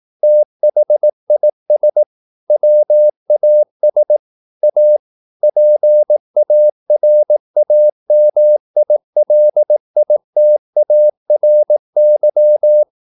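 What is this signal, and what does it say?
Morse code, heard as a keyed tone, about 18 wpm, 600 Hz. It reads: THIS WAS A PARAMILITARY
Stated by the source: Morse practice at 18 wpm, English text, from CW med Gnister.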